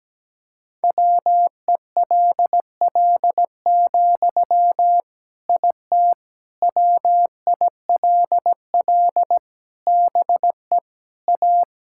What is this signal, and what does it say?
Morse code, heard as a keyed tone, about 17 wpm, 700 Hz. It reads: WELL, IT WILL BE A
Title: Sentences from Top 1000 Words - Review Part 6 - 17wpm